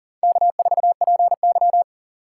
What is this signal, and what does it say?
Morse code, heard as a keyed tone, about 51 wpm, 700 Hz. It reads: K4PY